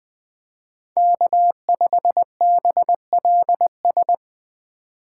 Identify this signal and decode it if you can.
Morse code, heard as a keyed tone, about 24 wpm, 700 Hz. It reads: K5BLS
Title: Speed Racing - USA 1x3 Call Signs 20wpm